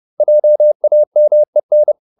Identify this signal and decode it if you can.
Morse code, heard as a keyed tone, about 30 wpm, 600 Hz. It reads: JAMEN
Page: Ferie 30 wpm